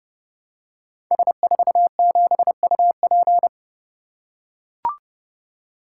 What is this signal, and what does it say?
Morse code, heard as a keyed tone, about 30 wpm, 700 Hz. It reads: S47UP
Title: Code Groups - 5 Letter:Numbers - 30wpm